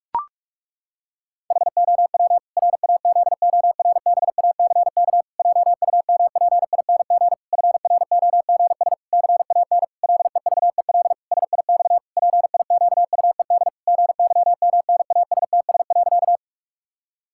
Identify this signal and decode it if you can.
Morse code, heard as a keyed tone, about 41 wpm, 700 Hz. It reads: HOW RAZORBACK JUMPING FROGS CAN LEVEL SIX PIQUED GYMNASTS.
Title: Pangram Challenge v2 - 45wpm